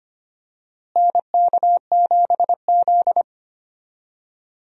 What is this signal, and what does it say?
Morse code, heard as a keyed tone, about 25 wpm, 700 Hz. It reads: NK7Z
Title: USA Call Signs - 4 Characters - Repeated 3x - 25wpm